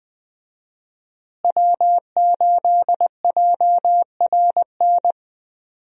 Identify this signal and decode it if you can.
Morse code, heard as a keyed tone, about 20 wpm, 700 Hz. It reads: W8JRN
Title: USA Call Signs - 5 Characters - Repeated 3x - 20wpm